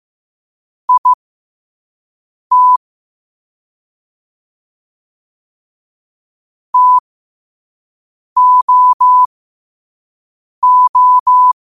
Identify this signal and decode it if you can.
Morse code, heard as a keyed tone, about 15 wpm, spaced as for 5 wpm, 1000 Hz. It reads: IT TOO
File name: qotdfiles / 5wpm